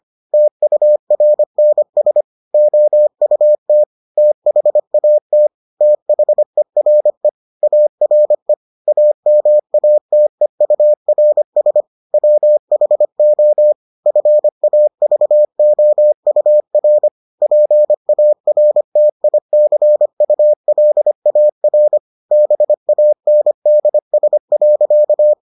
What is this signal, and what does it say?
Morse code, heard as a keyed tone, about 25 wpm, 600 Hz. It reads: TURNS OUT THAT THERE ARE AMATEURS WHO FAVOUR PARTICULAR BANDS.